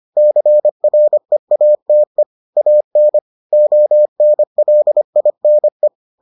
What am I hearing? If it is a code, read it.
Morse code, heard as a keyed tone, about 25 wpm, 600 Hz. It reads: CREATE AN ONLINE